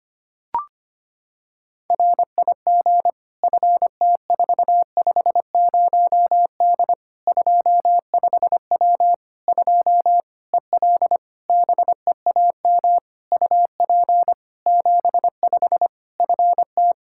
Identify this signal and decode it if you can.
Morse code, heard as a keyed tone, about 25 wpm, 700 Hz. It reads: RIG FT450D 25W 2 EL BEAM UP 75 FT
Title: Practice QSO - Exchange 2 - Info - 25wpm